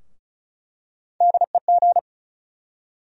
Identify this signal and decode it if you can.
Morse code, heard as a keyed tone, about 36 wpm, 700 Hz. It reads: DEG
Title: USA Call Sign Postfixes - 3 Letters 35wpm